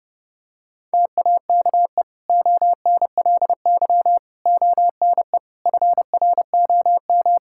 Morse code, heard as a keyed tone, about 30 wpm, 700 Hz. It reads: TAKE ONLY ONE FROM
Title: Sentences from Top 1000 Words - Review Part 5 - 30wpm